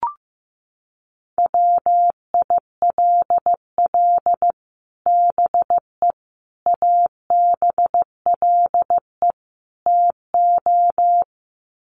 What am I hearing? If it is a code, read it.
Morse code, heard as a keyed tone, about 15 wpm, 700 Hz. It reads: WILL BE ABLE TO
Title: Sets of 4 Words 15wpm